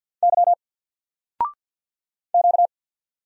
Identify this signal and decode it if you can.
Morse code, heard as a keyed tone, about 48 wpm, 700 Hz. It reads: Y Q
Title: Single Letter-Number 50wpm